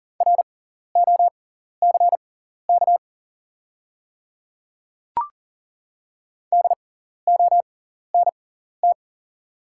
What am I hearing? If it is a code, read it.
Morse code, heard as a keyed tone, about 40 wpm, spaced as for 14 wpm, 700 Hz. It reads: ROCK DONT
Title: 6x Character Spacing - Top 500 Words - 40wpm